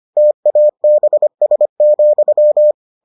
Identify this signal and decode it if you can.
Morse code, heard as a keyed tone, about 25 wpm, 600 Hz. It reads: TABS,